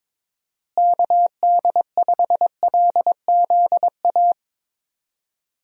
Morse code, heard as a keyed tone, about 22 wpm, 700 Hz. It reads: KD5LZA
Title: USA Call Signs - 6 Characters - Repeated 3x - 22wpm